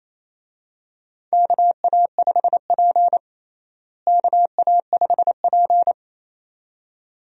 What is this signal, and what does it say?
Morse code, heard as a keyed tone, about 28 wpm, 700 Hz. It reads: KA5P KA5P
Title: USA Call Signs - 4 Characters - Repeated 2x - 28wpm